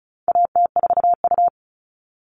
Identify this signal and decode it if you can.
Morse code, heard as a keyed tone, about 34 wpm, 700 Hz. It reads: AT4U
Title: All Cut Numbers - 4-Digit 35wpm